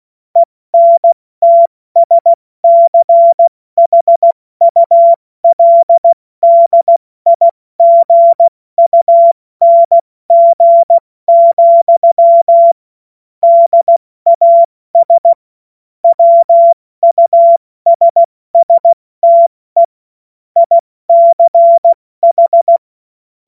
Morse code, heard as a keyed tone, about 16 wpm, spaced as for 14 wpm, 680 Hz. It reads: ENTSCHULDIGUNG, DAS WUSSTE ICH
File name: Morsekurs / Witze / Witz 13. Juli 2021, 70 F80